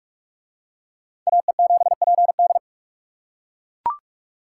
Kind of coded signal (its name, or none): Morse code